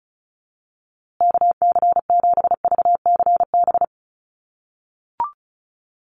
Morse code, heard as a keyed tone, about 34 wpm, 700 Hz. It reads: KC7VCB
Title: USA Call Signs Repeated 2x 35wpm